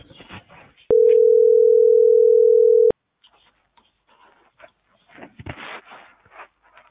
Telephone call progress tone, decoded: ringback tone

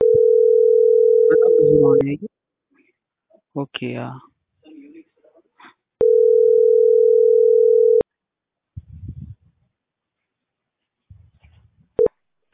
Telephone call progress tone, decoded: ringback tone

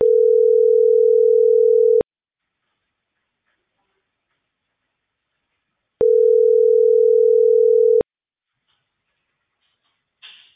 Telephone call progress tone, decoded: ringback tone

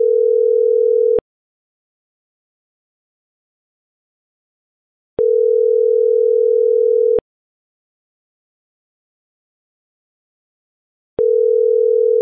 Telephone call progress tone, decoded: ringback tone

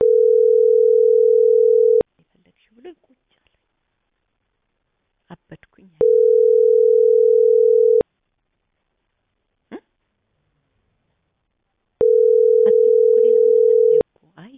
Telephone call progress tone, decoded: ringback tone

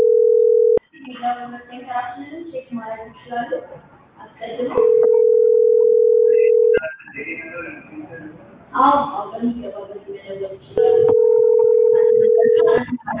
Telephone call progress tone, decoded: ringback tone

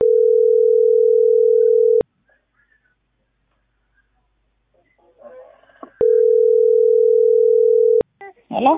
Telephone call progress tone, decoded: ringback tone

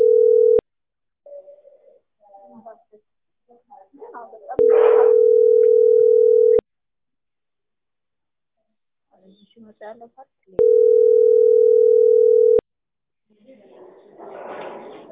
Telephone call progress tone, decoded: ringback tone

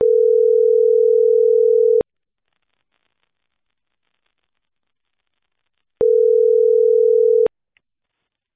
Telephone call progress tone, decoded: ringback tone

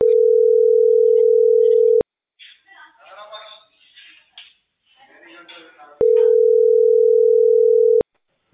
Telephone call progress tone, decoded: ringback tone

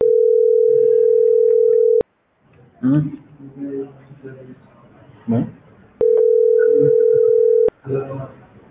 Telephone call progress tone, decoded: ringback tone

